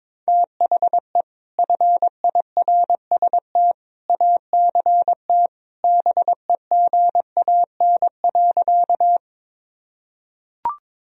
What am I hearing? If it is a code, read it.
Morse code, heard as a keyed tone, about 22 wpm, 700 Hz. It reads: THE FIRST ACT BEGAN.
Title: Sentences from Top 1000 Words - Review Part 1 - 22wpm